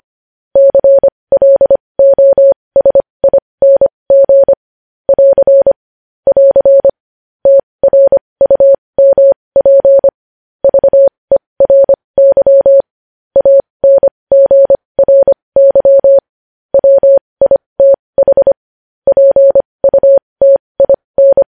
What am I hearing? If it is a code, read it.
Morse code, heard as a keyed tone, about 25 wpm, spaced as for 20 wpm, 550 Hz. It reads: CLOSING <AR> <AR> TRUMP VERY ANGRY WITH PUTIN
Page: news headlines in morse code at 20 wpm